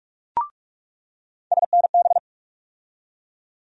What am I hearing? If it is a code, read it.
Morse code, heard as a keyed tone, about 47 wpm, 700 Hz. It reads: SNB
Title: USA Call Sign Postfixes - 3 Letters 45wpm